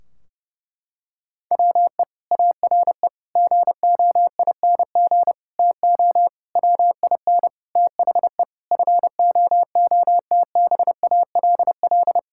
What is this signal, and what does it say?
Morse code, heard as a keyed tone, about 30 wpm, 700 Hz. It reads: WE ARE GOING TO WIN THE FOOTBALL
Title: Sentences from Top 1600 Words 30wpm